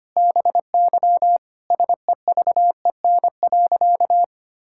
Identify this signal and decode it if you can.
Morse code, heard as a keyed tone, about 25 wpm, 700 Hz. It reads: BY SEVEN.
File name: practice / 25wpm / Sentences with 4-Word Phrases - 25wpm